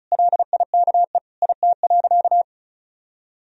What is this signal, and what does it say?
Morse code, heard as a keyed tone, about 36 wpm, 700 Hz. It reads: LIKE IT.
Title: Sentences from Top 1000 Words - Review Part 1 - 35wpm